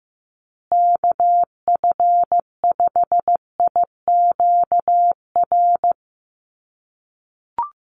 Morse code, heard as a keyed tone, about 15 wpm, 700 Hz. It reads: KF5IQR
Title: USA Call Signs Repeated 3x 15wpm